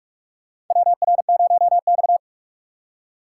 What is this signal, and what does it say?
Morse code, heard as a keyed tone, about 47 wpm, 700 Hz. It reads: WR0X
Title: USA Call Signs - 4x Character Spacing - 45wpm